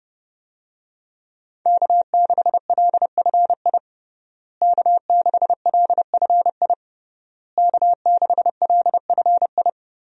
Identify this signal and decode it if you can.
Morse code, heard as a keyed tone, about 32 wpm, 700 Hz. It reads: K6LFI K6LFI K6LFI
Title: USA Call Signs - 5 Characters - Repeated 3x - 30wpm